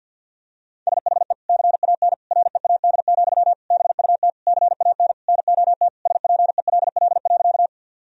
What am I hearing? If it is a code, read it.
Morse code, heard as a keyed tone, about 52 wpm, 700 Hz. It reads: SHE CAN READ, BUT CAN NOT SPELL.
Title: Sentences from Top 1000 Words - Review Part 1 - 50wpm